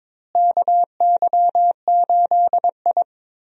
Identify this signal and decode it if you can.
Morse code, heard as a keyed tone, about 22 wpm, 700 Hz. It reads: KY8I
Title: USA Call Signs - 4 Characters - Repeated 3x - 22wpm